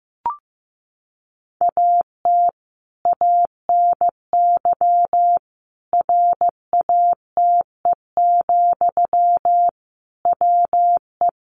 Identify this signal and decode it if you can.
Morse code, heard as a keyed tone, about 15 wpm, 700 Hz. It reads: AT ANY RATE, WE
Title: Sentences from Top 700 Words 15wpm